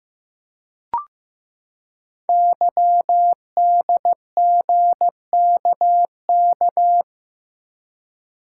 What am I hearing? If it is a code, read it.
Morse code, heard as a keyed tone, about 15 wpm, 700 Hz. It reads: YDGKK
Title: Code Groups - 5 Letters - 15wpm